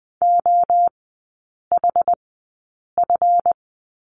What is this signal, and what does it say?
Morse code, heard as a keyed tone, about 20 wpm, spaced as for 8 wpm, 700 Hz. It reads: OHF
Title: 138 Lesson 23e - Letter f - Three Characters 20wpm